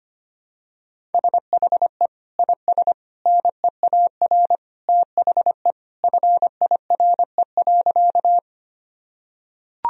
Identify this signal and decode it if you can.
Morse code, heard as a keyed tone, about 25 wpm, 700 Hz. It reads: SHE IS NEAR THE FIRE.
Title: Sentences from Top 1000 Words - Review Part 1 - 25wpm